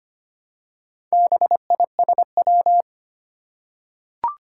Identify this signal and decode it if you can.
Morse code, heard as a keyed tone, about 25 wpm, 700 Hz. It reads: BISW
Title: Four Challenging Letter-Number Combinations 25wpm